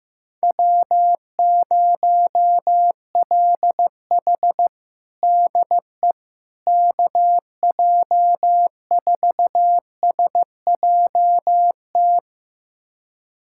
Morse code, heard as a keyed tone, about 16 wpm, 700 Hz. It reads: W0LH DE KJ4SJT
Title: USA Callsign DE Callsign 15wpm